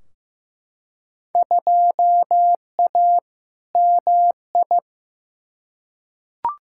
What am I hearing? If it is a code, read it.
Morse code, heard as a keyed tone, about 15 wpm, 700 Hz. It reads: 2A MI
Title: Field Day Exchanges Encore - 15wpm